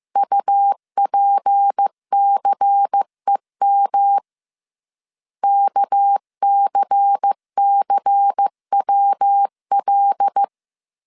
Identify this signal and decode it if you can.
Morse code, heard as a keyed tone, about 15 wpm, 800 Hz. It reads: UPCEM KCCWL